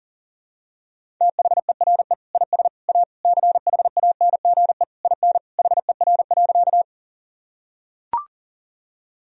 Morse code, heard as a keyed tone, about 40 wpm, 700 Hz. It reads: THERE IS A CHANGE IN HER.
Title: Sentences from Top 1000 Words - Review Part 1 - 40wpm